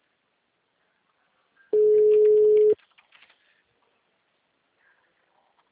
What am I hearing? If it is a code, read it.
Ringback tone (Europe, 425 Hz).